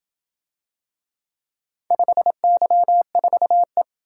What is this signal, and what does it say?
Morse code, heard as a keyed tone, about 27 wpm, 700 Hz. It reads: 5Y4E